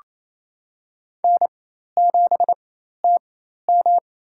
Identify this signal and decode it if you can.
Morse code, heard as a keyed tone, about 27 wpm, spaced as for 13 wpm, 700 Hz. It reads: N7TM
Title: USA Call Signs - 4 Characters - 4x Character Spacing - 28wpm